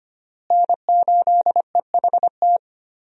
Morse code, heard as a keyed tone, about 25 wpm, 700 Hz. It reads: N8EHT
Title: USA Call Signs - 5 Characters - Repeated 3x - 25wpm